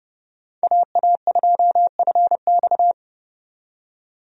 Morse code, heard as a keyed tone, about 32 wpm, 700 Hz. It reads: AA2FX
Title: USA Call Signs - 5 Characters - 30wpm